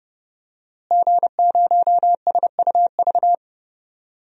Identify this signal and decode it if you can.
Morse code, heard as a keyed tone, about 32 wpm, 700 Hz. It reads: G0SUV
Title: DXCC-100 Call Signs - 5 Characters - Repeated 2x - 30wpm